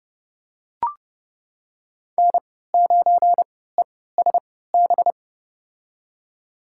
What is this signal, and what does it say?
Morse code, heard as a keyed tone, about 32 wpm, spaced as for 17 wpm, 700 Hz. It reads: N9ESB
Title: USA Call Signs - 5 Characters - 3x Character Spacing - 30wpm